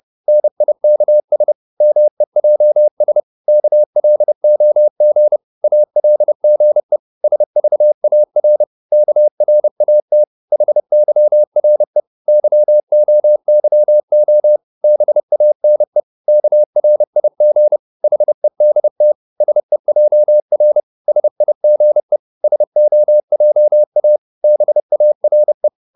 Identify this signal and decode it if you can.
Morse code, heard as a keyed tone, about 30 wpm, 600 Hz. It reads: NIKS MEJS KLOG ALGE SVAR KRAT HYRE YOYO BANE KRIG HEDT SEJR SIGE SOJA BARE
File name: Korte ord DK 30wpm